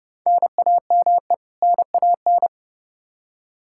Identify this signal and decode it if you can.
Morse code, heard as a keyed tone, about 30 wpm, 700 Hz. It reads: NAME NAN
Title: Practice QSO - Exchange 1 - Beginning - 30wpm